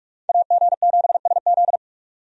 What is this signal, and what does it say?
Morse code, heard as a keyed tone, about 46 wpm, 700 Hz. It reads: AG7SZ
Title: USA Call Signs - 5 Characters - 45wpm